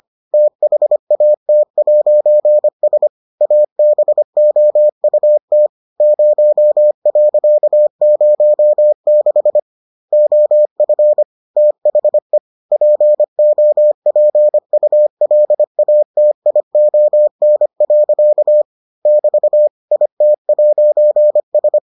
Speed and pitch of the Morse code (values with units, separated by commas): 25 wpm, 600 Hz